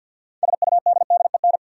Morse code, heard as a keyed tone, about 53 wpm, 700 Hz. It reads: SUDDEN
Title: Sets of 4 Words - New Word Drill - 50wpm